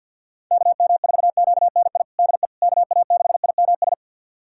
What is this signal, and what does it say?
Morse code, heard as a keyed tone, about 48 wpm, 700 Hz. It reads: KM4QNI DE KA6IMS
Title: USA Callsign DE Callsign - 6 Characters - 50wpm